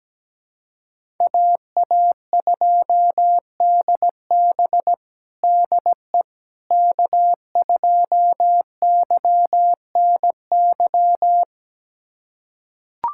Morse code, heard as a keyed tone, about 17 wpm, 700 Hz. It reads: AA2DB DE K2YNY
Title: USA Callsign DE Callsign - 5 Characters - 17wpm